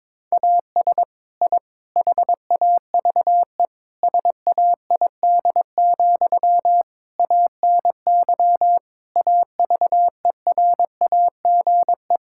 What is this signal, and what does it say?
Morse code, heard as a keyed tone, about 22 wpm, 700 Hz. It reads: AS I HAVE SAID, ANY AVERAGE
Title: Sentences from Top 1000 Words - Review Part 6 - 22wpm